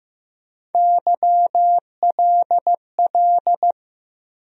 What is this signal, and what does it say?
Morse code, heard as a keyed tone, about 15 wpm, 700 Hz. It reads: YLL
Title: USA Call Sign Postfixes - 3 Letters 15wpm